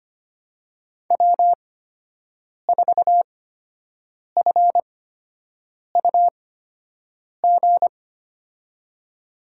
Morse code, heard as a keyed tone, about 25 wpm, spaced as for 7 wpm, 700 Hz. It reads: W4FUG
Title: USA Call Signs - 8x Character Spacing - 25wpm